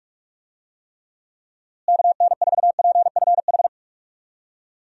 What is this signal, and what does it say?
Morse code, heard as a keyed tone, about 46 wpm, 700 Hz. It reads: KN4PFH
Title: USA Call Signs - 6 Characters - Repeated 3x - 45wpm